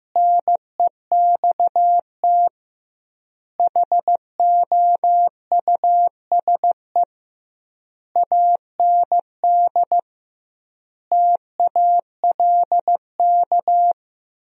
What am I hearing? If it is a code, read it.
Morse code, heard as a keyed tone, about 16 wpm, 700 Hz. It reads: NEXT HOUSE AND TALK
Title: 2x Word-Spacing - Sentences from Top 400 Words - 15wpm